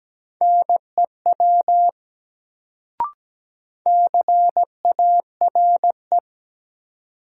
Morse code, heard as a keyed tone, about 17 wpm, 700 Hz. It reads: NEW CARE
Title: Sentences from Top 300 Words - New Word Drill - 17wpm